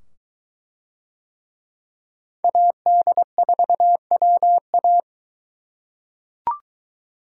Morse code, heard as a keyed tone, about 23 wpm, 700 Hz. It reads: AD4WA